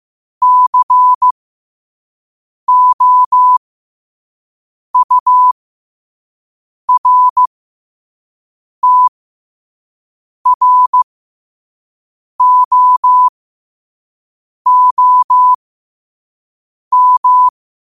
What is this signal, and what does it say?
Morse code, heard as a keyed tone, about 15 wpm, spaced as for 5 wpm, 1000 Hz. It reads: COURTROOM
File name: qotdfiles / 5wpm